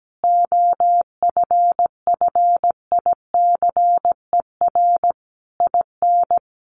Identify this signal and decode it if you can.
Morse code, heard as a keyed tone, about 17 wpm, 700 Hz. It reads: OFFICER IN
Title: Sentences from Top 1200 Words 17wpm